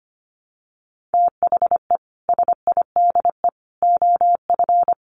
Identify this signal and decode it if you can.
Morse code, heard as a keyed tone, about 25 wpm, 700 Hz. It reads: THE SIDE OF
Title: Sets of 3 Words 25wpm